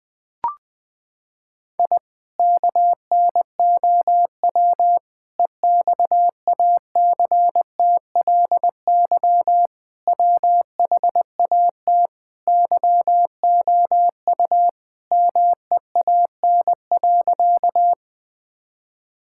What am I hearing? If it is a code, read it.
Morse code, heard as a keyed tone, about 21 wpm, 700 Hz. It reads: I KNOW EXACTLY WHAT YOU MEAN.
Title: Sentences from Top 1000 Words - Review Part 6 - 20wpm